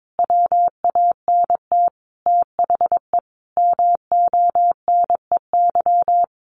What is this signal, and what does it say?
Morse code, heard as a keyed tone, about 22 wpm, 700 Hz. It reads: WANT THE MONEY
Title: Sentences from Top 400 Words 22wpm